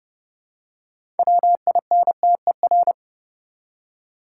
Morse code, heard as a keyed tone, about 30 wpm, 700 Hz. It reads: WINTER
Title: Six Letter Words - 30wpm